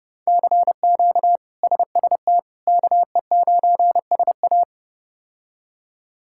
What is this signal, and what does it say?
Morse code, heard as a keyed tone, about 30 wpm, 700 Hz. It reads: CQ SST KE9SA
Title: K1USN SST QSO Practice - 30wpm